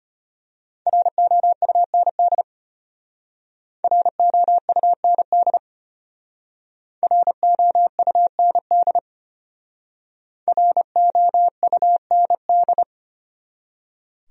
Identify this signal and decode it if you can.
Morse code, heard as a keyed tone, about 34 wpm, 700 Hz. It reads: ROUND ROUND ROUND ROUND